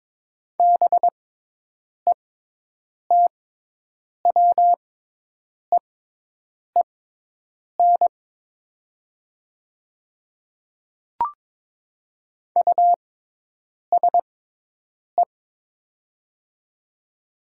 Morse code, heard as a keyed tone, about 22 wpm, spaced as for 8 wpm, 700 Hz. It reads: BETWEEN USE
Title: 6x Character Spacing - Top 300 Words - 22wpm